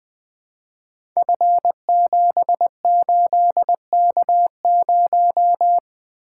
Morse code, heard as a keyed tone, about 20 wpm, 700 Hz. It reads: F78K0
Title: Code Groups - 5 Letter:Numbers - 20wpm